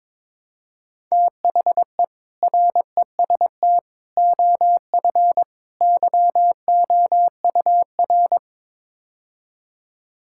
Morse code, heard as a keyed tone, about 22 wpm, 700 Hz. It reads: THE REST OF YOUR
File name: practice / 22wpm / Build A Phrase - 5-Word Phrases - 22wpm